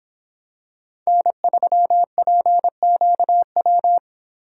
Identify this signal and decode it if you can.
Morse code, heard as a keyed tone, about 23 wpm, 700 Hz. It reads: N3PQW